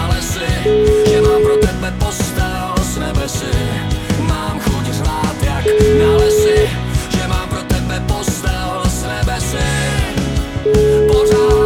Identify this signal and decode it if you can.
Ringback tone (Europe, 425 Hz), with noise.